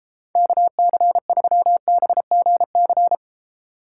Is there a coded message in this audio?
Yes: Morse code from 0.4 to 3.1 s, heard as a keyed tone, about 33 wpm, 700 Hz. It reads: KC3BGC